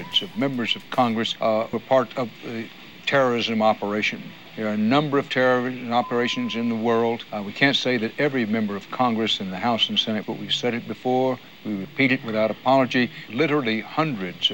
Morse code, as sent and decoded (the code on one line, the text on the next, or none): none